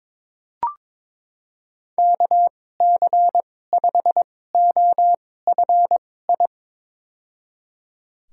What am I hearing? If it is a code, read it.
Morse code, heard as a keyed tone, about 22 wpm, spaced as for 16 wpm, 700 Hz. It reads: KC5OFI
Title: USA Call Signs - 6 Characters - 2x Character Spacing - 22wpm